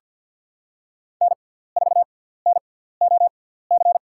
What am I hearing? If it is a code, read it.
Morse code, heard as a keyed tone, about 51 wpm, spaced as for 17 wpm, 700 Hz. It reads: N4NOC
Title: USA Call Signs - 5 Characters - 6x Character Spacing - 50wpm